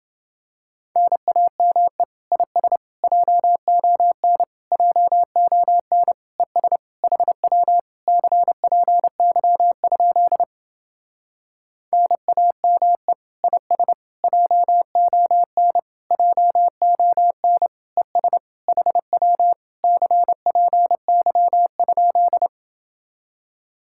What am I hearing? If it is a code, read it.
Morse code, heard as a keyed tone, about 29 wpm, 700 Hz. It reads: NAME IS JON JON ES HW CPY? NAME IS JON JON ES HW CPY?